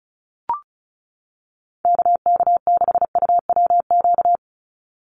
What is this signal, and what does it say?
Morse code, heard as a keyed tone, about 36 wpm, 700 Hz. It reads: KK6UWQ
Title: USA Call Signs 35wpm